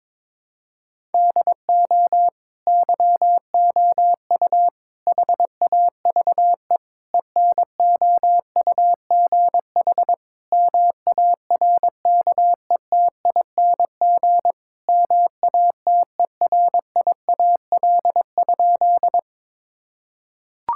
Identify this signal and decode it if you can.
Morse code, heard as a keyed tone, about 22 wpm, 700 Hz. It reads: DO YOU HAVE ENOUGH MARKETING MATERIAL?
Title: Sentences from Top 1000 Words - Review Part 5 - 22wpm